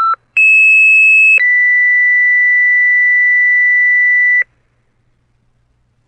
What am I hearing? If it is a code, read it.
Two-tone page, heard as a one-second tone, then a three-second tone, about 2600 then 2000 Hz, with noise. It is heard from 0.4 to 4.4 s.
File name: fire-pager-ringtone